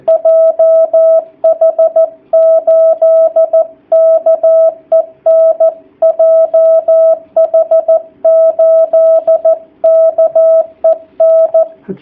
Morse code, heard as a keyed tone, about 14 wpm, 660 Hz. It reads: JH8KEN JH8KEN
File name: jh8cbh / 2019cw / sp13- w